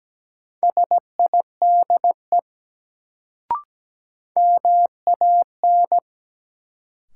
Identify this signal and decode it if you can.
Morse code, heard as a keyed tone, about 17 wpm, 700 Hz. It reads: SIDE MAN
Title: Sets of 2 Words - New Word Drill - 17wpm